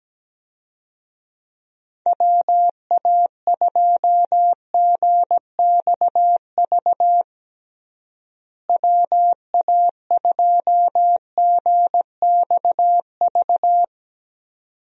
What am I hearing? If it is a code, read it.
Morse code, heard as a keyed tone, about 17 wpm, 700 Hz. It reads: WA2GXV WA2GXV
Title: USA Call Signs - 6 Characters - Repeated 2x - 17wpm